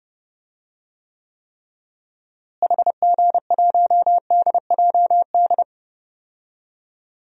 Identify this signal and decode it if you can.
Morse code, heard as a keyed tone, about 30 wpm, 700 Hz. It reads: HG1DJD
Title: DXCC-100 Call Signs - 6 Characters - Repeated 3x - 30wpm